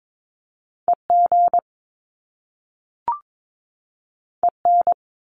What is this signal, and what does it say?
Morse code, heard as a keyed tone, about 22 wpm, 700 Hz. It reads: EG EN
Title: Top Two Letter Combinations in Words 22wpm